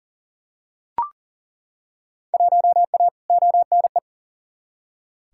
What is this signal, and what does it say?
Morse code, heard as a keyed tone, about 40 wpm, 700 Hz. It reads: 1A ONE
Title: Field Day Exchanges - 40wpm